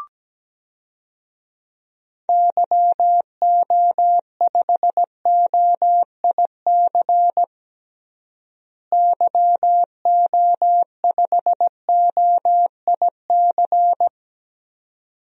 Morse code, heard as a keyed tone, about 17 wpm, 700 Hz. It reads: YO5OIC YO5OIC
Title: DXCC-100 Call Signs - 6 Characters - Repeated 2x - 17wpm